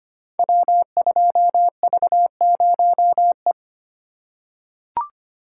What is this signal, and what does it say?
Morse code, heard as a keyed tone, about 25 wpm, 700 Hz. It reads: W2V0E
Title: Code Groups - 5 Letter:Numbers - 25wpm